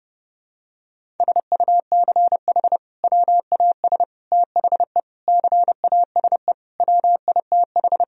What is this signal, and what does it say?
Morse code, heard as a keyed tone, about 30 wpm, 700 Hz. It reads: SUCH WAS THE CASE WITH
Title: Sentences from Top 1000 Words - Review Part 1 - 30wpm